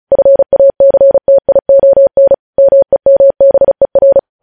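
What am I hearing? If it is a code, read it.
Morse code, heard as a keyed tone, about 36 wpm, 550 Hz. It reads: FACTION MEMBER
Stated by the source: Morse code news headlines at 35.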